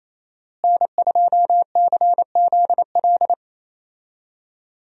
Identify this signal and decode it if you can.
Morse code, heard as a keyed tone, about 28 wpm, 700 Hz. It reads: N2CZL
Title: USA Call Signs - 5 Characters - Repeated 2x - 28wpm